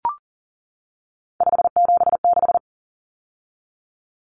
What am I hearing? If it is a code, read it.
Morse code, heard as a keyed tone, about 40 wpm, 700 Hz. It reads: 576
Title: Numbers - 3-Digit 40wpm